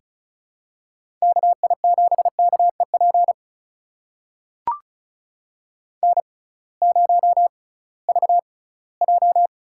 Morse code, heard as a keyed tone, about 34 wpm, spaced as for 20 wpm, 700 Hz. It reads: KI7KEP N0VJ
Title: USA Call Signs - 6x Character Spacing - 35wpm